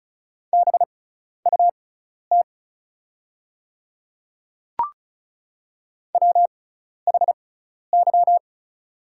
Morse code, heard as a keyed tone, about 36 wpm, spaced as for 12 wpm, 700 Hz. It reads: BUT WHY
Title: 6x Character Spacing - Top 200 Words - 35wpm